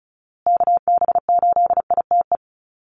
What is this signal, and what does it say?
Morse code, heard as a keyed tone, about 34 wpm, 700 Hz. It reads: KB8ITE
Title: USA Call Signs Repeated 2x 35wpm